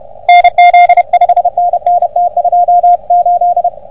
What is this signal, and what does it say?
Morse code, heard as a keyed tone, about 32 wpm, 720 Hz, with radio noise. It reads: NZ 5NN T28
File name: OM3PA PILUP CUC 388